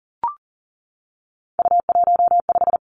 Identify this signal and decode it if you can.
Morse code, heard as a keyed tone, about 39 wpm, 700 Hz. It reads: U15
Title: Three Challenging Letter-Number Combinations 40wpm